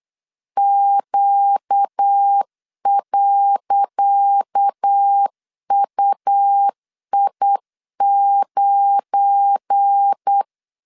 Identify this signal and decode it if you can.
Morse code, heard as a keyed tone, about 8 wpm, 800 Hz. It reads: Q.UI9
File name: LEZIONI / 10 SIGNS / 10C DOT 8wpm 3649